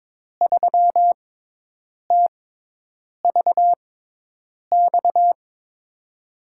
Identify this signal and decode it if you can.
Morse code, heard as a keyed tone, about 22 wpm, spaced as for 8 wpm, 700 Hz. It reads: 3TVX
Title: USA Call Signs - 6 Characters - 6x Character Spacing - 22wpm